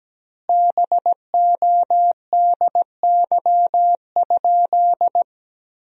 Morse code, heard as a keyed tone, about 17 wpm, 700 Hz. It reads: BODY?
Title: 2x Word-Spacing - Sentences from Top 400 Words - 17wpm